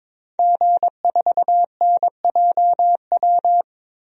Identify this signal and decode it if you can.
Morse code, heard as a keyed tone, about 22 wpm, 700 Hz. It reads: G4NJW
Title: DXCC-100 Call Signs - 5 Characters - 22wpm